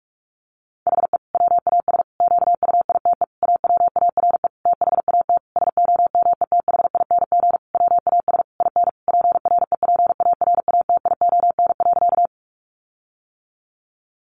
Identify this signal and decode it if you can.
Morse code, heard as a keyed tone, about 44 wpm, 700 Hz. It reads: HE WAS QUITE AWARE THAT SOMETHING WAS IN PREPARATION.
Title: Sentences from Top 1400 Words 45wpm